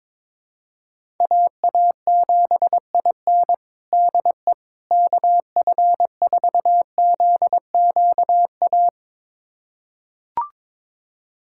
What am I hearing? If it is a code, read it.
Morse code, heard as a keyed tone, about 22 wpm, 700 Hz. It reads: AA7IN DE KF4ZQA
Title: USA Callsign DE Callsign 22wpm